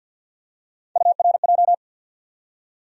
Morse code, heard as a keyed tone, about 53 wpm, 700 Hz. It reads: URJ
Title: USA Call Sign Postfixes - 3 Letters 50wpm